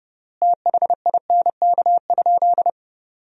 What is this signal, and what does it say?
Morse code, heard as a keyed tone, about 30 wpm, 700 Hz. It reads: THINK?
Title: 3x Word-Spacing - Sentences from Top 200 Words - 30wpm